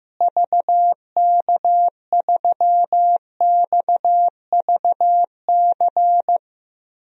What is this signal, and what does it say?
Morse code, heard as a keyed tone, about 15 wpm, 700 Hz. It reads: VK3XVC
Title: DXCC-100 Call Signs - 6 Characters - Repeated 3x - 15wpm